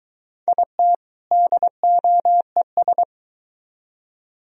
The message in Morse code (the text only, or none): IT DOES